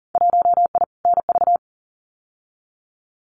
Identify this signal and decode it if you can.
Morse code, heard as a keyed tone, about 40 wpm, 700 Hz. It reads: 1I NV
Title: Winter Field Day Exchanges Encore 40wpm